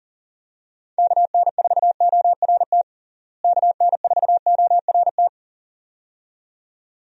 Morse code, heard as a keyed tone, about 40 wpm, 700 Hz. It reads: KN4ORT KN4ORT
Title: USA Call Signs - 6 Characters - Repeated 2x - 40wpm